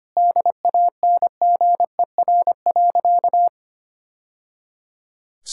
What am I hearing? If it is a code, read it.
Morse code, heard as a keyed tone, about 25 wpm, 690 Hz. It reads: DANGER.
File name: practice / 25wpm / Sentences with 2-Word Phrases - 25wpm